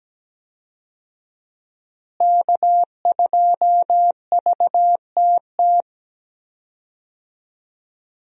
Morse code, heard as a keyed tone, about 17 wpm, 700 Hz. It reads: K2VTT
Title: USA Call Signs - 5 Characters - Repeated 2x - 17wpm